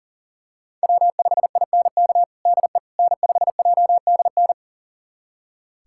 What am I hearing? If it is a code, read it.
Morse code, heard as a keyed tone, about 40 wpm, 700 Hz. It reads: W5INK DE N5JDN
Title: USA Callsign DE Callsign - 5 Characters - 40wpm